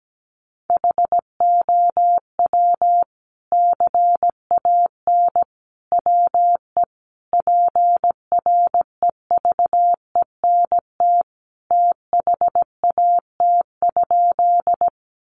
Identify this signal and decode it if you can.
Morse code, heard as a keyed tone, about 17 wpm, 700 Hz. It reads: HOW CAN WE PREVENT THAT?
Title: Sentences from Top 1100 Words 17wpm